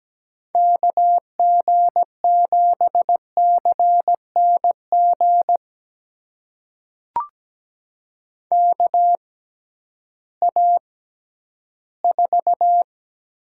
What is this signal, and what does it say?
Morse code, heard as a keyed tone, about 17 wpm, spaced as for 11 wpm, 700 Hz. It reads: KG7CNG KA4
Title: USA Call Signs - 6 Characters - 6x Character Spacing - 17wpm